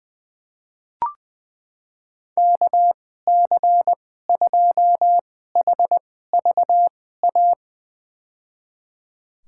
Morse code, heard as a keyed tone, about 20 wpm, spaced as for 15 wpm, 700 Hz. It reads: KC2HVA
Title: USA Call Signs - 6 Characters - 2x Character Spacing - 20wpm